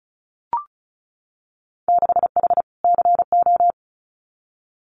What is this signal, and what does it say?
Morse code, heard as a keyed tone, about 36 wpm, 700 Hz. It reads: 6H CO
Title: Winter Field Day Exchanges Encore 35wpm